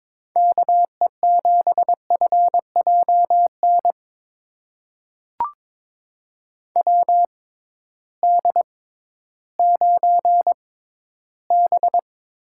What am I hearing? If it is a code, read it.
Morse code, heard as a keyed tone, about 22 wpm, spaced as for 13 wpm, 700 Hz. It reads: KE7FJN WD9B
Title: USA Call Signs - 6 Characters - 6x Character Spacing - 22wpm